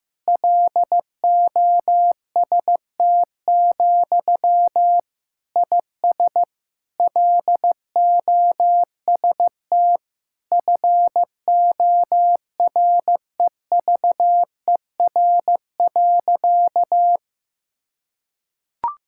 Morse code, heard as a keyed tone, about 15 wpm, 700 Hz. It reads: LOST, IS LOST FOREVER.
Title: Sentences from Top 1600 Words 15wpm